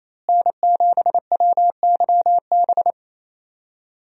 Morse code, heard as a keyed tone, about 28 wpm, 700 Hz. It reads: N7WYB
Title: USA Call Signs - 5 Characters - Repeated 3x - 28wpm